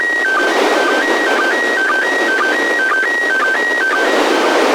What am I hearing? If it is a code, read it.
SSTV (slow-scan television), PD 120.
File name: posts / receiving-sstv-images-from-iss / iss sstv1